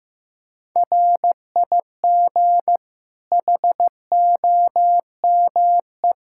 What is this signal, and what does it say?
Morse code, heard as a keyed tone, about 15 wpm, 700 Hz. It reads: RIG HOME
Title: Practice QSO - Exchange 2 - Info - 15wpm